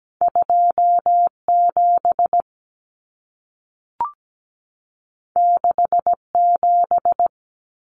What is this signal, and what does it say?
Morse code, heard as a keyed tone, about 17 wpm, 700 Hz. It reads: 27 67
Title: Numbers - 2-Digit 17wpm